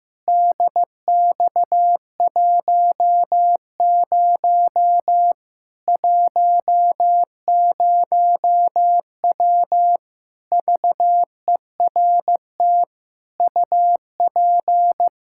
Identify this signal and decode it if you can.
Morse code, heard as a keyed tone, about 15 wpm, 700 Hz. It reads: DX10 10W VERT UP
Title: Practice QSO - Exchange 2 - Info - 15wpm